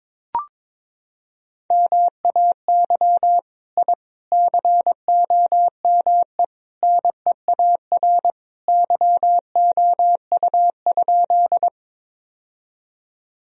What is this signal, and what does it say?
Morse code, heard as a keyed tone, about 22 wpm, 700 Hz. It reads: MAY I COME NEAR YOU?
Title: Sentences from Top 1000 Words - Review Part 1 - 22wpm